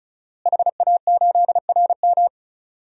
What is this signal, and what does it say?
Morse code, heard as a keyed tone, about 34 wpm, 700 Hz. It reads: HA8RM
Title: DXCC-100 Call Signs - 5 Characters - Repeated 2x - 35wpm